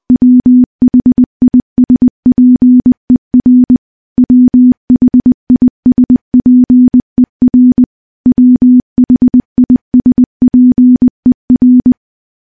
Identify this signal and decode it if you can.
Morse code, heard as a keyed tone, about 20 wpm, 250 Hz. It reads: WHISPER WHISPER WHISPER
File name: morse-training-20WPM